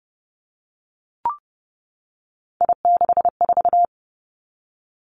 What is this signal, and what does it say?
Morse code, heard as a keyed tone, about 32 wpm, 700 Hz. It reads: I64
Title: Three Challenging Letter-Number Combinations 30wpm